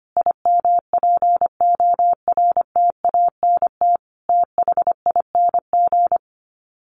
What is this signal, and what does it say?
Morse code, heard as a keyed tone, about 25 wpm, 700 Hz. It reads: IMPORTANT THING
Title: Sets of 4 Words 25wpm